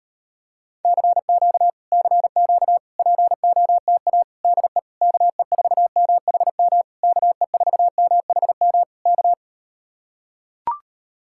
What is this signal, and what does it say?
Morse code, heard as a keyed tone, about 39 wpm, 700 Hz. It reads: CQ CQ POTA DE KE4MHM KE4MHM K